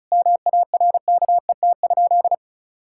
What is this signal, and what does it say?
Morse code, heard as a keyed tone, about 34 wpm, 700 Hz. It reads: MARKET?
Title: Sentences from Top 1000 Words - Review Part 5 - 35wpm